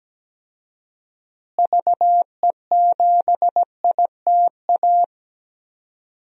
Morse code, heard as a keyed tone, about 17 wpm, 700 Hz. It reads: VE7ITA
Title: DXCC-100 Call Signs - 6 Characters - Repeated 2x - 17wpm